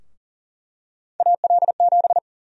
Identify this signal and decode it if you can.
Morse code, heard as a keyed tone, about 40 wpm, 700 Hz. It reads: AL7
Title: USA Call Sign Prefixes - Repeated 2x - 40wpm